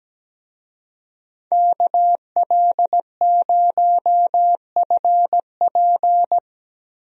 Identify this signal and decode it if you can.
Morse code, heard as a keyed tone, about 17 wpm, 700 Hz. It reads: KL0FP
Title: USA Call Signs - 5 Characters - Repeated 2x - 17wpm